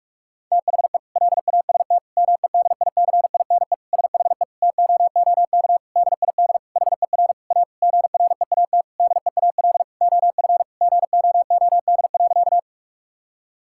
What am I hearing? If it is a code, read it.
Morse code, heard as a keyed tone, about 47 wpm, 700 Hz. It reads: THE LAST MEDICINE SHE TOOK DID HER A GREAT DEAL OF GOOD.
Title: Sentences from Top 1000 Words - Review Part 6 - 45wpm